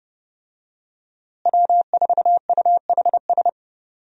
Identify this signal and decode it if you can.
Morse code, heard as a keyed tone, about 30 wpm, 700 Hz. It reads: W4UHS
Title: USA Call Signs - 5 Characters - Repeated 2x - 30wpm